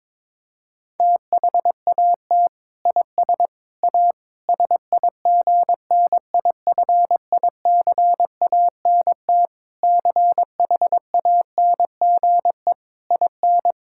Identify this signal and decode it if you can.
Morse code, heard as a keyed tone, about 22 wpm, 700 Hz. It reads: THAT IS A SIGNIFICANT CHANGE IN
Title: Sentences from Top 1000 Words - Review Part 6 - 22wpm